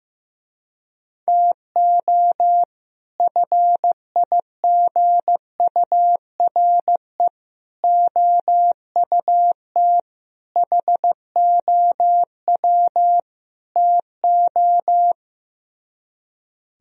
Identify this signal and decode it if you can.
Morse code, heard as a keyed tone, about 15 wpm, 700 Hz. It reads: TO FIGURE OUT HOW TO
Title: Speed-Racing - Sets of 5 Words 15wpm